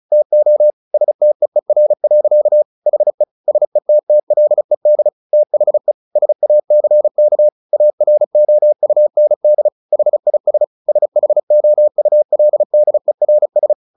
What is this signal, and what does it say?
Morse code, heard as a keyed tone, about 34 wpm, 600 Hz. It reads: TO STEER. HE SETTLED THE SACK AROUND HIS SHOULDERS
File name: the old man and the sea 2 35wpm